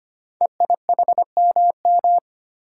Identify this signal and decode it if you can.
Morse code, heard as a keyed tone, about 25 wpm, 700 Hz. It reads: EIHMM
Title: Code Groups - 5 Letters - 25wpm